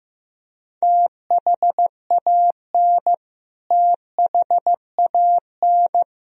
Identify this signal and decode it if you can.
Morse code, heard as a keyed tone, about 15 wpm, 700 Hz. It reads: THAN THAN
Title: Top 100 Words Repeated 15wpm